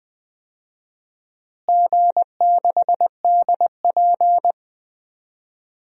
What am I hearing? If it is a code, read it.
Morse code, heard as a keyed tone, about 20 wpm, 700 Hz. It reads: G6DP
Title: DXCC-100 Call Signs - 4 Characters - Repeated 2x - 20wpm